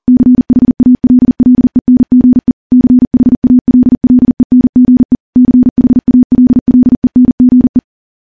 Morse code, heard as a keyed tone, about 40 wpm, 250 Hz. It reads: CHALLENGE CHALLENGE CHALLENGE
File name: morse-training-40WPM